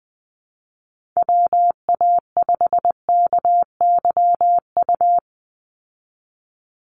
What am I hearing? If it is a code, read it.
Morse code, heard as a keyed tone, about 20 wpm, 700 Hz. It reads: WA5KYU
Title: USA Call Signs 20wpm